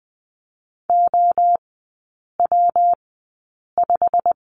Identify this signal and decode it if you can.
Morse code, heard as a keyed tone, about 20 wpm, spaced as for 8 wpm, 700 Hz. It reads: OW5
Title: 172 Lesson 28e - Number 9 - Three Characters 20wpm